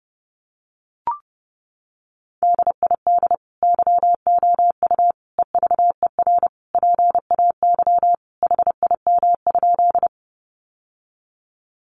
Morse code, heard as a keyed tone, about 30 wpm, 700 Hz. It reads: DID YOU EVER PAY HIM?
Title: Sentences from Top 700 Words 30wpm